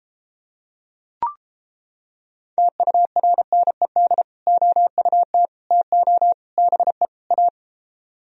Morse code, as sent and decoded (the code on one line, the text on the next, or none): - ..- .-. -. -.. / --- ..- - / - --- / -... / .-
TURNED OUT TO BE A